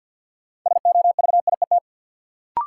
Morse code, heard as a keyed tone, about 51 wpm, 700 Hz. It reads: SOVIET
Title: Sets of 5 Words Encore - New Word Drill - 50wpm